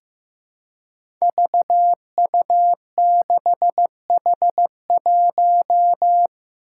Morse code, heard as a keyed tone, about 15 wpm, 700 Hz. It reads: VU6H1
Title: Five Challenging Letter-Number Combinations 15wpm